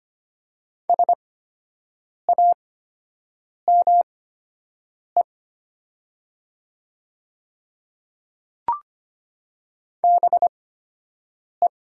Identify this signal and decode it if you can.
Morse code, heard as a keyed tone, about 25 wpm, spaced as for 7 wpm, 700 Hz. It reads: SAME BE
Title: 8x Character Spacing - Top 200 Words - 25wpm